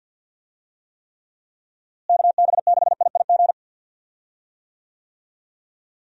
Morse code, heard as a keyed tone, about 47 wpm, 700 Hz. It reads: KB6IIG
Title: USA Call Signs - 6 Characters - Repeated 2x - 50wpm